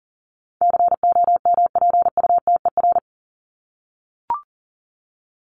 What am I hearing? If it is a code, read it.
Morse code, heard as a keyed tone, about 40 wpm, 700 Hz. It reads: COMPUTER
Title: Top 501-750 Words 40wpm